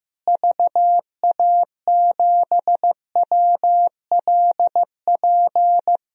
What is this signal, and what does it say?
Morse code, heard as a keyed tone, about 15 wpm, 700 Hz. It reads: VA7WLP
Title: DXCC-100 Call Signs - 6 Characters - Repeated 2x - 15wpm